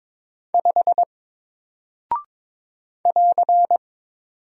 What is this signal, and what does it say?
Morse code, heard as a keyed tone, about 22 wpm, 700 Hz. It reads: 5 <AR>
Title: Single Character (Letter:Number:Punctuation:Prosign) Alt 22wpm